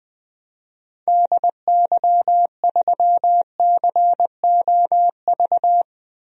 Morse code, heard as a keyed tone, about 21 wpm, 700 Hz. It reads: DY3COV
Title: DXCC-100 Call Signs - 6 Characters - 20wpm